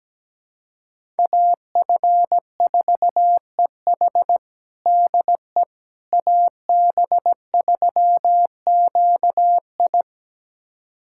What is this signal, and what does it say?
Morse code, heard as a keyed tone, about 17 wpm, 700 Hz. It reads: AF4EH DE AB3QI
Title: USA Callsign DE Callsign - 5 Characters - 17wpm